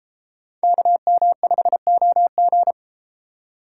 Morse code, heard as a keyed tone, about 32 wpm, 700 Hz. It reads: KM5OG